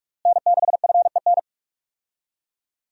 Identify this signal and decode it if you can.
Morse code, heard as a keyed tone, about 47 wpm, 700 Hz. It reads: N6FEN
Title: USA Call Signs - 5 Characters - Repeated 3x - 45wpm